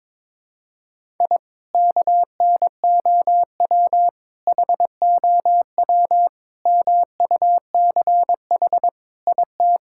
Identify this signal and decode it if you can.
Morse code, heard as a keyed tone, about 21 wpm, 700 Hz. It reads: I KNOW HOW MUCH IT
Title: Sentences from Top 1000 Words - Review Part 5 - 22wpm